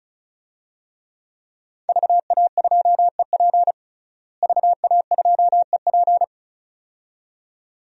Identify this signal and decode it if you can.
Morse code, heard as a keyed tone, about 34 wpm, 700 Hz. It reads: VA2EP VA2EP
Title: DXCC-100 Call Signs - 5 Characters - Repeated 2x - 35wpm